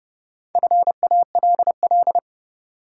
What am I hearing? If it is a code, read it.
Morse code, heard as a keyed tone, about 30 wpm, 700 Hz. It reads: FALL
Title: Sentences from Top 500 Words - New Word Drill - 30wpm